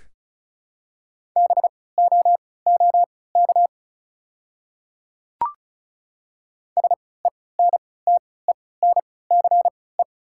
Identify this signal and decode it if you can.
Morse code, heard as a keyed tone, about 34 wpm, spaced as for 20 wpm, 690 Hz. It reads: BOOK SENTENCE
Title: 3x Character Spacing - Top 300 Words - 35wpm